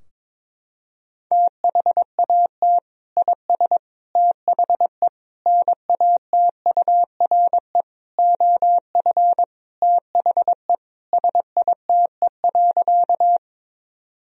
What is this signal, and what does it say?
Morse code, heard as a keyed tone, about 22 wpm, 700 Hz. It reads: THAT IS THE NATURE OF THE SITE.
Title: Sentences from Top 1000 Words - Review Part 4 - 22wpm